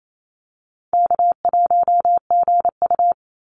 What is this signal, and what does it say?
Morse code, heard as a keyed tone, about 28 wpm, 700 Hz. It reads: K1GU
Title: Call Signs Repeated 2x - 28wpm